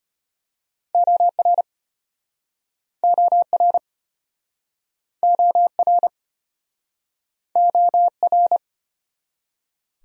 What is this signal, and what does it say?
Morse code, heard as a keyed tone, about 33 wpm, 700 Hz. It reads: OR OR OR OR